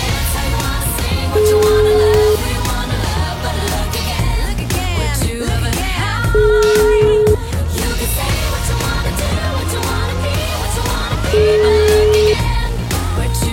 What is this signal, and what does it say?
Ringback tone (Europe, 425 Hz), with noise.